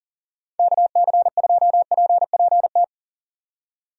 Morse code, heard as a keyed tone, about 39 wpm, 700 Hz. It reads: KC2PPT
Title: USA Call Signs - 6 Characters - 40wpm